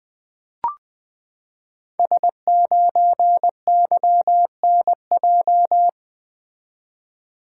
Morse code, heard as a keyed tone, about 20 wpm, 700 Hz. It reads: S9YNJ
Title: Code Groups - 5 Letter:Numbers - 20wpm